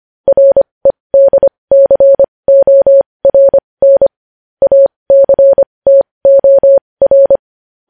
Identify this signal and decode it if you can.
Morse code, heard as a keyed tone, about 25 wpm, spaced as for 20 wpm, 550 Hz. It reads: REDCORN ACTOR
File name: podcast / News headlines-1x-25x20